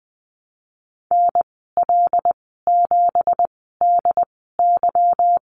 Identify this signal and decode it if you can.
Morse code, heard as a keyed tone, about 20 wpm, spaced as for 15 wpm, 700 Hz. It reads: NL7DY
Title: 167 Lesson 27g - Number 7 - Call Signs 20wpm